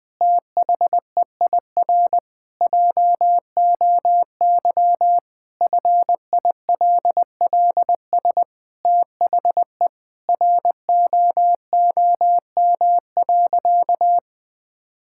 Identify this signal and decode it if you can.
Morse code, heard as a keyed tone, about 20 wpm, 700 Hz. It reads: THEIR JOY FILLS THE ROOM.
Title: Sentences with 2-Word Phrases - 20wpm